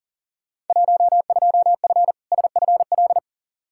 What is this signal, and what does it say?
Morse code, heard as a keyed tone, about 40 wpm, 700 Hz. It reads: 12F SFL
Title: Field Day Exchanges Encore - 40wpm